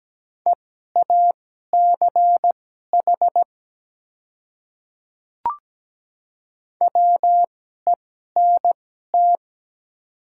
Morse code, heard as a keyed tone, about 17 wpm, spaced as for 12 wpm, 700 Hz. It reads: EACH WENT
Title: 2x Character Spacing - Top 200 Words - 17wpm